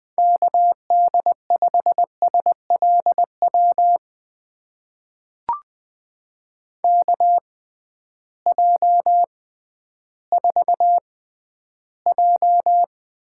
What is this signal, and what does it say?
Morse code, heard as a keyed tone, about 20 wpm, spaced as for 12 wpm, 700 Hz. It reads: KD5SLW KJ4J
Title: USA Call Signs - 6 Characters - 6x Character Spacing - 20wpm